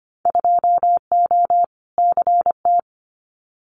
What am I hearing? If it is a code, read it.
Morse code, heard as a keyed tone, about 25 wpm, 700 Hz. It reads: 2O CT
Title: Winter Field Day Exchanges 25wpm